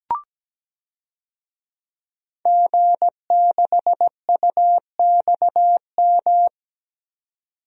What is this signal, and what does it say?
Morse code, heard as a keyed tone, about 17 wpm, 700 Hz. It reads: G6UXM
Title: DXCC-100 Call Signs - 5 Characters - 17wpm